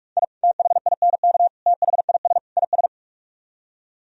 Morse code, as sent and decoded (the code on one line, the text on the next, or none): .. / - .... .. -. -.- / - .... .. ... / .. ...
I THINK THIS IS